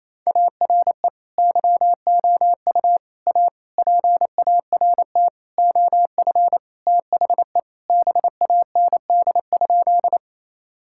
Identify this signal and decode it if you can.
Morse code, heard as a keyed tone, about 27 wpm, 700 Hz. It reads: ARE YOU A PART OF THE BAND?
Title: Sentences from Top 1600 Words 28wpm